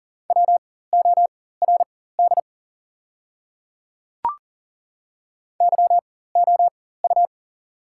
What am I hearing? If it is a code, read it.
Morse code, heard as a keyed tone, about 37 wpm, spaced as for 18 wpm, 690 Hz. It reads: WORD YOU
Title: 4x Character Spacing - Top 100 Words - 40wpm